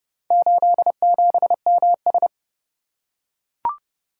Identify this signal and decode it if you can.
Morse code, heard as a keyed tone, about 26 wpm, 700 Hz. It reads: 87MS